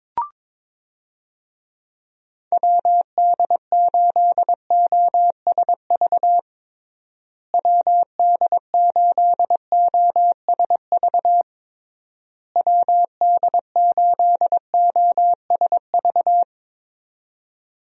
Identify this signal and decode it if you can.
Morse code, heard as a keyed tone, about 22 wpm, 700 Hz. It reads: WD8OSV WD8OSV WD8OSV
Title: USA Call Signs - 6 Characters - Repeated 3x - 22wpm